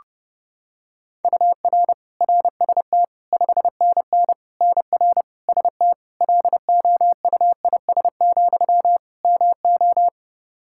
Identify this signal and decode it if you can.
Morse code, heard as a keyed tone, about 32 wpm, 700 Hz. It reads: UR RST 5NN NR ST LOUIS, MO
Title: Practice QSO - Exchange 1 - Beginning - 30wpm